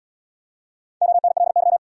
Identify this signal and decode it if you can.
Morse code, heard as a keyed tone, about 76 wpm, 700 Hz. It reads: ONLY